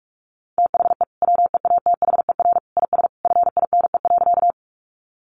Morse code, heard as a keyed tone, about 44 wpm, 700 Hz. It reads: THE WEATHER IS FINE.
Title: Sentences from Top 800 Words 45wpm